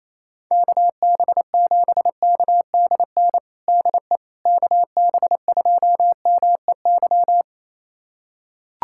Morse code, heard as a keyed tone, about 28 wpm, 700 Hz. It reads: KB7KDN DE KB2MEY
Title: USA Callsign DE Callsign - 6 Characters - 28wpm